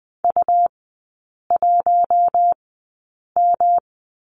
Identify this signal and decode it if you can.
Morse code, heard as a keyed tone, about 20 wpm, spaced as for 8 wpm, 700 Hz. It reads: U1M
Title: 138 Lesson 23e - Letter f - Three Characters 20wpm